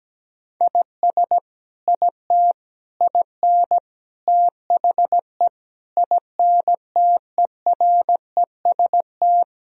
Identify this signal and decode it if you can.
Morse code, heard as a keyed tone, about 17 wpm, 700 Hz. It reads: IS IT IN THE INTEREST
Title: Sentences from Top 1000 Words - Review Part 4 - 17wpm